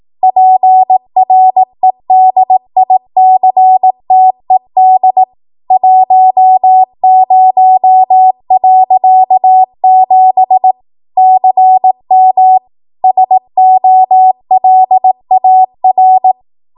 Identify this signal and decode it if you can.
Morse code, heard as a keyed tone, about 18 wpm, 750 Hz. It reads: PREDICTED 10.7 CM SOLAR